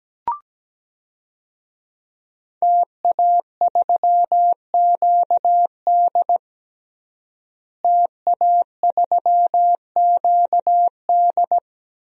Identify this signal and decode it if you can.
Morse code, heard as a keyed tone, about 17 wpm, 700 Hz. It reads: TA3QD TA3QD
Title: DXCC-100 Call Signs - 5 Characters - Repeated 3x - 17wpm